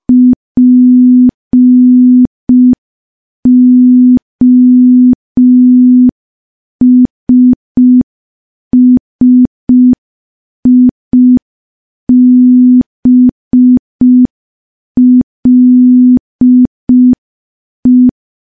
Morse code, heard as a keyed tone, about 5 wpm, 250 Hz. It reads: POSSIBLE
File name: morse / morse-training-05WPM